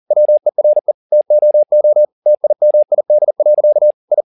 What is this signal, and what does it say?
Morse code, heard as a keyed tone, about 40 wpm, 600 Hz. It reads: WERE TOO TIMID. I